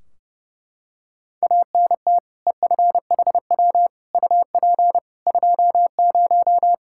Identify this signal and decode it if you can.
Morse code, heard as a keyed tone, about 30 wpm, 700 Hz. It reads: ANT EFHW UP 20
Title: Practice QSO - Exchange 2 - Info - 30wpm